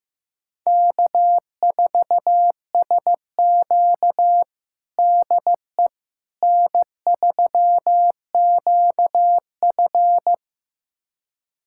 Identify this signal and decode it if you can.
Morse code, heard as a keyed tone, about 15 wpm, 700 Hz. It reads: K4SQ DE N3QF
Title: USA Callsign DE Callsign - 4 Characters - 15wpm